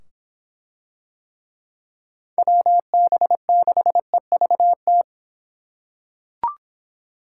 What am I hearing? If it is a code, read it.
Morse code, heard as a keyed tone, about 25 wpm, 700 Hz. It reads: WB6EVT